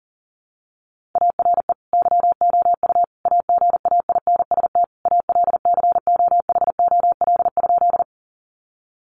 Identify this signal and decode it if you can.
Morse code, heard as a keyed tone, about 37 wpm, 700 Hz. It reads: ARE YOU AGAINST ALCOHOL?
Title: Sentences from Top 1400 Words 40wpm